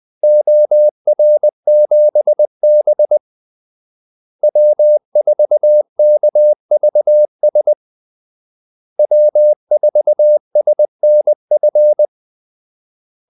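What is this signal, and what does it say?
Morse code, heard as a keyed tone, about 20 wpm, 600 Hz. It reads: OR7B W4KVS W4SNF
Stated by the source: Morse call-sign practice at 20 wpm.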